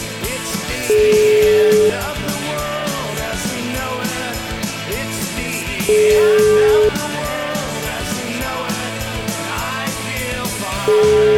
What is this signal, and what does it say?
Ringback tone (Europe, 425 Hz), with noise.